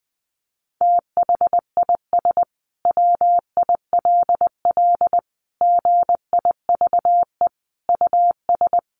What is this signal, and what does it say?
Morse code, heard as a keyed tone, about 20 wpm, 700 Hz. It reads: THIS WILL GIVE US
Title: Sentences from Top 1300 Words 20wpm